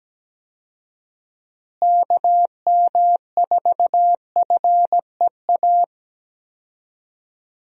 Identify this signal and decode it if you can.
Morse code, heard as a keyed tone, about 17 wpm, 700 Hz. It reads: KM4FEA
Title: USA Call Signs - 6 Characters - Repeated 3x - 17wpm